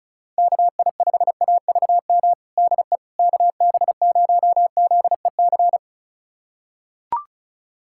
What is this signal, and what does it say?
Morse code, heard as a keyed tone, about 36 wpm, 700 Hz. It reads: KI5AVM DE KB0ZEC
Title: USA Callsign DE Callsign - 6 Characters - 35wpm